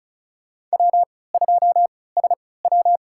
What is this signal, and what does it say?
Morse code, heard as a keyed tone, about 34 wpm, spaced as for 20 wpm, 700 Hz. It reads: W2SW
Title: USA Call Signs - 4 Characters - 3x Character Spacing - 35wpm